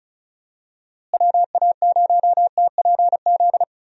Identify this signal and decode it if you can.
Morse code, heard as a keyed tone, about 36 wpm, 700 Hz. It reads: WA0TPZ
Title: USA Call Signs - 6 Characters - Repeated 2x - 35wpm